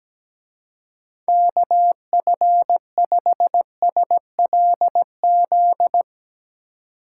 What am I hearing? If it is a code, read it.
Morse code, heard as a keyed tone, about 18 wpm, 700 Hz. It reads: KF5SLZ
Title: Speed Racing - USA 2x3 Call Signs 17wpm